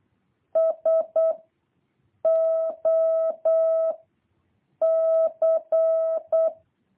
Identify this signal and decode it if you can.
Morse code, heard as a keyed tone, about 8 wpm, 650 Hz. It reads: S O C
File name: jh8cbh / 2023cw